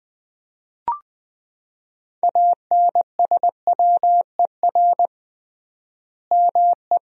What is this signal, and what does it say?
Morse code, heard as a keyed tone, about 20 wpm, 700 Hz. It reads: ANSWER ME
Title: 3x Word-Spacing - Sentences from Top 500 Words - 20wpm